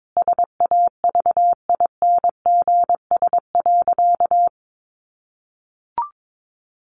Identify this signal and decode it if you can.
Morse code, heard as a keyed tone, about 22 wpm, 700 Hz. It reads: SAVINGS.
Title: Sentences from Top 1400 Words 22wpm